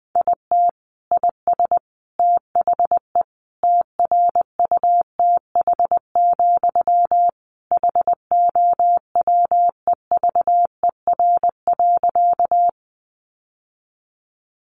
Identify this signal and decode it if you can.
Morse code, heard as a keyed tone, about 19 wpm, 700 Hz. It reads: IT IS THE TRUTH, HOWEVER.
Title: Sentences from Top 800 Words 20wpm